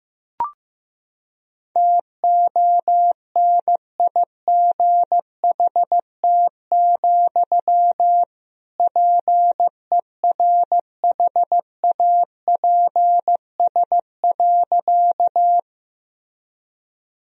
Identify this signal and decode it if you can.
Morse code, heard as a keyed tone, about 15 wpm, 700 Hz. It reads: TONIGHT, PERHAPS.
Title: Sentences from Top 1000 Words - Review Part 3 - 15wpm